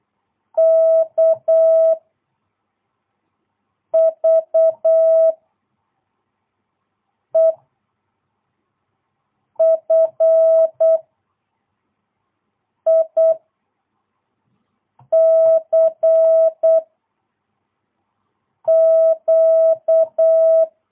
Morse code, heard as a keyed tone, about 8 wpm, 660 Hz. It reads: K V E F I C Q
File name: jh8cbh / 2022cw